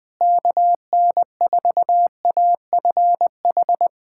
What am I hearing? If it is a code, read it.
Morse code, heard as a keyed tone, about 21 wpm, 700 Hz. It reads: KN4AFH
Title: USA Callsign DE Callsign 20wpm